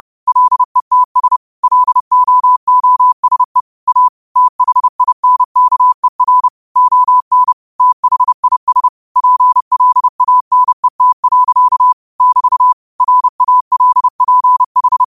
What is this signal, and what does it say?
Morse code, heard as a keyed tone, about 30 wpm, 1000 Hz. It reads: LETS LOOSE A THINKER ON THIS PLANET. <BT> RALPH